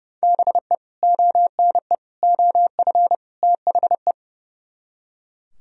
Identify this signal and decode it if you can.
Morse code, heard as a keyed tone, about 40 wpm, spaced as for 35 wpm, 700 Hz. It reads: BE ONE OF THE
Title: Speed-Racing - Sets of 4 Words 30wpm